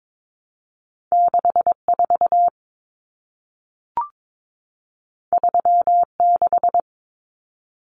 Morse code, heard as a keyed tone, about 22 wpm, 700 Hz. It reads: 64 36
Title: Common Cut Numbers - 2-Digit 22wpm